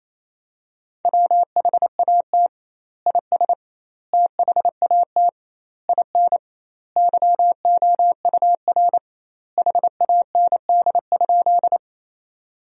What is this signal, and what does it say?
Morse code, heard as a keyed tone, about 28 wpm, 700 Hz. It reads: WHAT IS THAT IN YOUR HAND?
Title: 2x Word-Spacing - Sentences from Top 200 Words - 28wpm